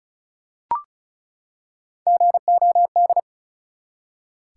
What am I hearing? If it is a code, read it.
Morse code, heard as a keyed tone, about 36 wpm, 700 Hz. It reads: GOD
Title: Three Letter Words - 35wpm